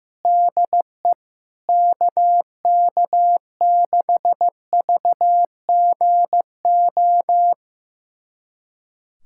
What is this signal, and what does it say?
Morse code, heard as a keyed tone, about 15 wpm, 700 Hz. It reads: DE KK6VGO
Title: USA Callsign DE Callsign - 6 Characters - 15wpm